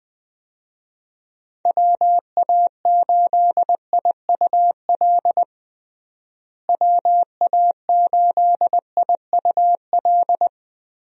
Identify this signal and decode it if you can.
Morse code, heard as a keyed tone, about 21 wpm, 700 Hz. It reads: WA8IUL WA8IUL
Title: USA Call Signs - 6 Characters - Repeated 3x - 20wpm